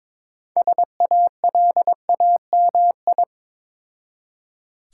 Morse code, heard as a keyed tone, about 22 wpm, 690 Hz. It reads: SALAMI
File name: Difficult Words with Hints - 22wpm